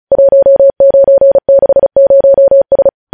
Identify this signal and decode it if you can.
Morse code, heard as a keyed tone, about 36 wpm, 550 Hz. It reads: 1960S
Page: news headlines in morse code at 35 wpm